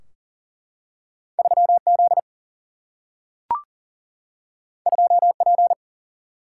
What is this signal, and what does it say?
Morse code, heard as a keyed tone, about 40 wpm, 700 Hz. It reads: 3Z 2P
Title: Single-Letter Single-Number 40wpm